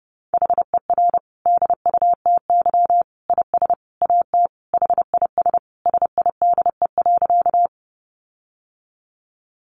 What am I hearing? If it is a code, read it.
Morse code, heard as a keyed tone, about 30 wpm, 700 Hz. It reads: HER DUTY IS AT HIS SIDE.
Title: Sentences from Top 1500 Words 30wpm